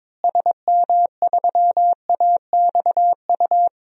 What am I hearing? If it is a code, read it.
Morse code, heard as a keyed tone, about 21 wpm, 700 Hz. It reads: SM3AXU
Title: DXCC-100 Call Signs - 6 Characters - Repeated 3x - 22wpm